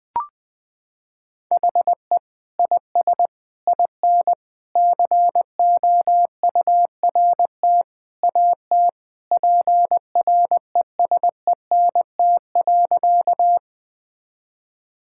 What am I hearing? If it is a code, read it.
Morse code, heard as a keyed tone, about 21 wpm, 700 Hz. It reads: HE IS IN COURT AT PRESENT.
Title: Sentences from Top 1600 Words 20wpm